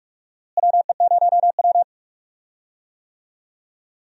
Morse code, heard as a keyed tone, about 45 wpm, 700 Hz. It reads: WE0W
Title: USA Call Signs - 4 Characters - Repeated 3x - 45wpm